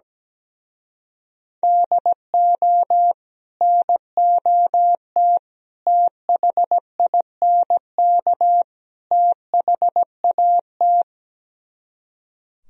Morse code, heard as a keyed tone, about 17 wpm, 700 Hz. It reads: DO NOT THINK THAT
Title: Speed-Racing - Sets of 4 Words 17wpm